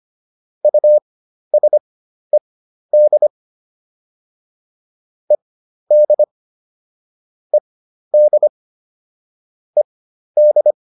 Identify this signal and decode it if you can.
Morse code, heard as a keyed tone, about 25 wpm, spaced as for 12 wpm, 600 Hz. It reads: USED ED ED ED